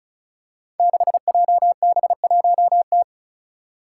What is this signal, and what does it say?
Morse code, heard as a keyed tone, about 36 wpm, 700 Hz. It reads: 6JB1T
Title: Five Challenging Letter-Number Combinations 35wpm